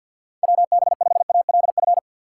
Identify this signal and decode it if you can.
Morse code, heard as a keyed tone, about 52 wpm, 700 Hz. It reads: WB5ALF
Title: USA Call Signs - 4x Character Spacing - 50wpm